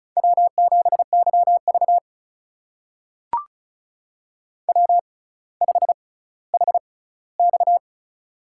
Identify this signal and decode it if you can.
Morse code, heard as a keyed tone, about 36 wpm, spaced as for 18 wpm, 700 Hz. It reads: W7YV W5HX
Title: USA Call Signs - 4 Characters - 6x Character Spacing - 35wpm